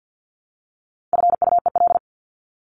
Morse code, heard as a keyed tone, about 50 wpm, 700 Hz. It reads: FUEL
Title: Top 1751-2000 Words 50wpm